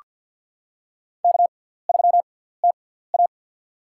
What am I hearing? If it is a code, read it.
Morse code, heard as a keyed tone, about 46 wpm, spaced as for 17 wpm, 700 Hz. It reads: K3TA
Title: USA Call Signs - 6x Character Spacing - 50wpm